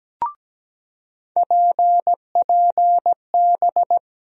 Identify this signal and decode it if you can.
Morse code, heard as a keyed tone, about 17 wpm, 700 Hz. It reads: PPB
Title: USA Call Sign Postfixes - 3 Letters 17wpm